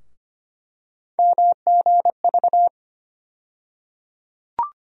Morse code, heard as a keyed tone, about 25 wpm, 700 Hz. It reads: MGV
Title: USA Call Sign Postfixes - 3 Letters 25wpm